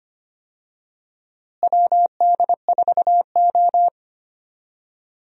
Morse code, heard as a keyed tone, about 25 wpm, 700 Hz. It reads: WD4O
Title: USA Call Signs - 4 Characters - Repeated 3x - 25wpm